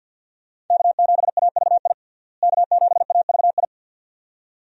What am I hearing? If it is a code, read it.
Morse code, heard as a keyed tone, about 50 wpm, 700 Hz. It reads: K7AVI K7AVI
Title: USA Call Signs - 5 Characters - Repeated 2x - 50wpm